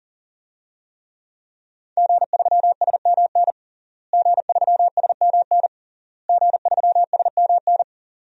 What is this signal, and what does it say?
Morse code, heard as a keyed tone, about 37 wpm, 700 Hz. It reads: G3SMN G3SMN G3SMN
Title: DXCC-100 Call Signs - 5 Characters - Repeated 3x - 40wpm